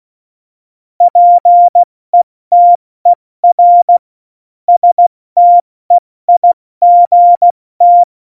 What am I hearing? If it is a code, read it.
Morse code, heard as a keyed tone, about 16 wpm, 700 Hz. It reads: PETER STEIGT
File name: Morsekurs / Witze / Witz 12. Februar 2019, 70 F80